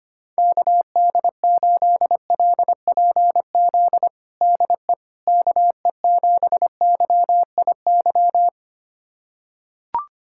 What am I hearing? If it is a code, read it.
Morse code, heard as a keyed tone, about 25 wpm, 700 Hz. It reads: KD8LPZ DE KE7YIY
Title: USA Callsign DE Callsign - 6 Characters - 25wpm